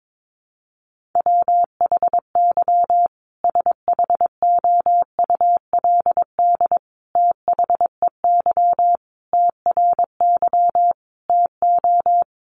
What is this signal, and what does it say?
Morse code, heard as a keyed tone, about 22 wpm, 700 Hz. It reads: WHY SHOULD THEY TRY TO
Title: Sentences from Top 1100 Words 22wpm